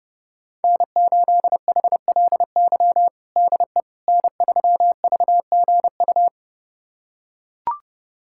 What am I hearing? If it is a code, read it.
Morse code, heard as a keyed tone, about 30 wpm, 700 Hz. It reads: N8HLY DE N3VGU
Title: USA Callsign DE Callsign - 5 Characters - 30wpm